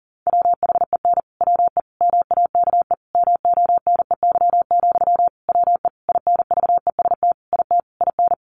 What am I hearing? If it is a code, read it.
Morse code, heard as a keyed tone, about 40 wpm, 700 Hz. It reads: WHEN WE MAKE MONEY, WE INVEST IT IN
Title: Sentences from Top 1100 Words 40wpm